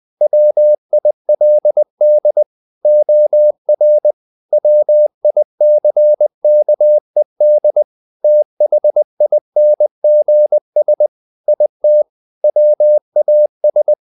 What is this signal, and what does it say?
Morse code, heard as a keyed tone, about 20 wpm, 600 Hz. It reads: WILD OR WICKED THINGS IT WAS
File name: the old man and the sea / the old man and the sea part 1 20wpm